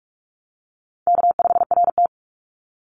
Morse code, heard as a keyed tone, about 44 wpm, 700 Hz. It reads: K5RT
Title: Easy Call Signs - 45wpm